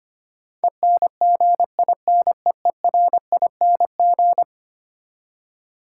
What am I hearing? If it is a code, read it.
Morse code, heard as a keyed tone, about 25 wpm, 700 Hz. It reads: ENGINEERING
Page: morse-code-ninja-practice-25wpm